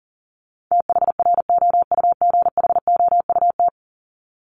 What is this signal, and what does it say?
Morse code, heard as a keyed tone, about 37 wpm, 700 Hz. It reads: THROUGHOUT
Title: Top 751-1000 Words 40wpm